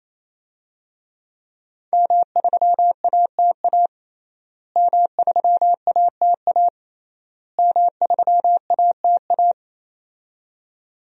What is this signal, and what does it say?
Morse code, heard as a keyed tone, about 28 wpm, 700 Hz. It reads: M3ATA M3ATA M3ATA
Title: DXCC-100 Call Signs - 5 Characters - Repeated 3x - 28wpm